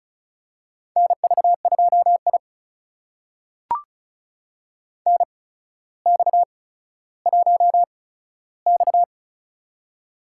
Morse code, heard as a keyed tone, about 36 wpm, spaced as for 15 wpm, 700 Hz. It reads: NV2I NX1X
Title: USA Call Signs - 4 Characters - 8x Character Spacing - 35wpm